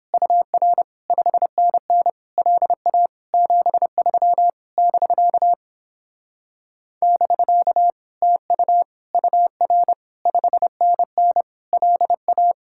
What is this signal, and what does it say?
Morse code, heard as a keyed tone, about 29 wpm, 700 Hz. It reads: UR 5NN LA 73 <BK> <BK> TU UR 5NN LA